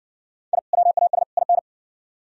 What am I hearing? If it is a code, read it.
Morse code, heard as a keyed tone, about 53 wpm, 700 Hz. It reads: I WAS IN